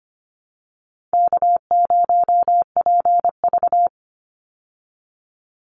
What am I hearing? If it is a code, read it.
Morse code, heard as a keyed tone, about 25 wpm, 700 Hz. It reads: K0PV
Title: Call Signs Repeated 2x - 25wpm